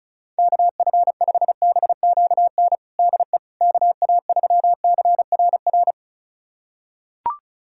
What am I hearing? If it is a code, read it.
Morse code, heard as a keyed tone, about 36 wpm, 700 Hz. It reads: KF5BQN DE KA3CRR
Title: USA Callsign DE Callsign - 6 Characters - 35wpm